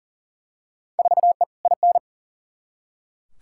Morse code, heard as a keyed tone, about 40 wpm, 700 Hz. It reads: 4E IN
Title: Field Day Exchanges - 40wpm